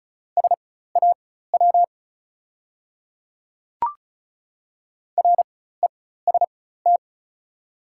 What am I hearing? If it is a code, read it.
Morse code, heard as a keyed tone, about 33 wpm, spaced as for 16 wpm, 690 Hz. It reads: SAW REST
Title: 4x Character Spacing - Top 500 Words - 35wpm